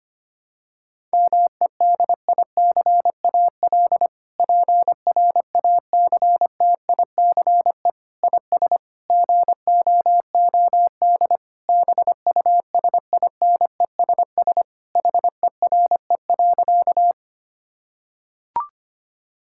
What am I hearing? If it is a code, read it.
Morse code, heard as a keyed tone, about 25 wpm, 700 Hz. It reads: MEDICAL PRACTICE IS GOOD BUSINESS HERE.
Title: Sentences from Top 1000 Words - Review Part 5 - 25wpm